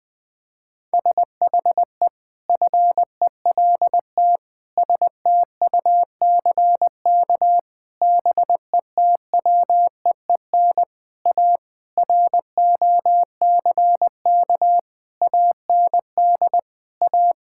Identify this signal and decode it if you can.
Morse code, heard as a keyed tone, about 20 wpm, 700 Hz. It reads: SHE FELT STUCK BETWEEN A ROCK AND A
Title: Sentences with 8-Word Phrases - Spoken Beginning - 20wpm